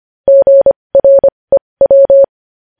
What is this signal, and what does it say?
Morse code, heard as a keyed tone, about 25 wpm, spaced as for 20 wpm, 550 Hz. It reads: GREW
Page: news headlines in morse code at 20 wpm